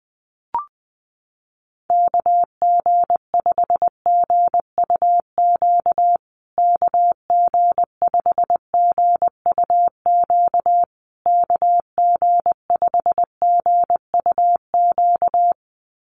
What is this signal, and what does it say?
Morse code, heard as a keyed tone, about 20 wpm, 700 Hz. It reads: KG5GUQ KG5GUQ KG5GUQ
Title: USA Call Signs Repeated 3x 20wpm